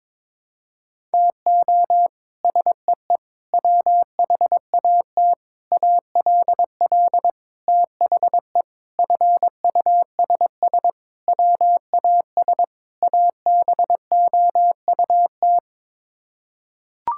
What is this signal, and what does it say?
Morse code, heard as a keyed tone, about 22 wpm, 700 Hz. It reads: TO SEE WHAT ALL THE FUSS WAS ABOUT
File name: practice / 22wpm / Sets of 8 Words - 22wpm